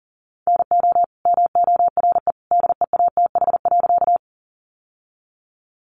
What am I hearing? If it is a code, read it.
Morse code, heard as a keyed tone, about 37 wpm, 700 Hz. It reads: NO MORE DEATH.
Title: Sentences from Top 1100 Words 40wpm